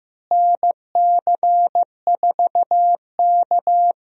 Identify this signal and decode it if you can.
Morse code, heard as a keyed tone, about 15 wpm, 700 Hz. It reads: NC4K
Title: USA Call Signs - 4 Characters - Repeated 2x - 15wpm